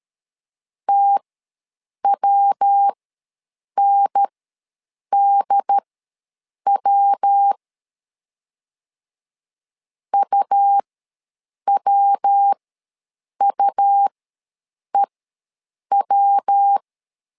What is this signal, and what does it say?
Morse code, heard as a keyed tone, about 13 wpm, spaced as for 7 wpm, 800 Hz. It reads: TWNDW UWUEW
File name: LEZIONI / 3 MWRO / 3 MW ws 12wpm 1816